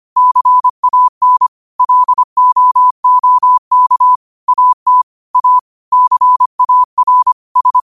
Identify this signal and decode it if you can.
Morse code, heard as a keyed tone, about 25 wpm, 1000 Hz. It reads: CAN LOOK AT A CAR S